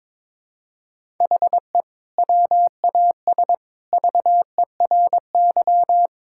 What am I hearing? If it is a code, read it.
Morse code, heard as a keyed tone, about 21 wpm, 700 Hz. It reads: HE WAS VERY
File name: practice / 22wpm / Sentences with 2-Word Phrases - 22wpm